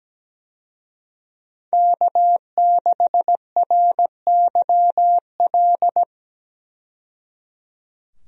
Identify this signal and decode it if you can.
Morse code, heard as a keyed tone, about 17 wpm, 700 Hz. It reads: K6RYL
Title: USA Call Signs - 5 Characters - 17wpm